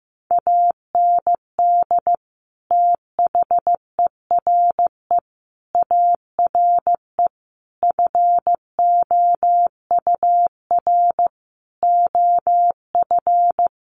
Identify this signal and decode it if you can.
Morse code, heard as a keyed tone, about 15 wpm, 700 Hz. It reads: AND THERE ARE FOUR OF
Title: Sentences from Top 300 Words 15wpm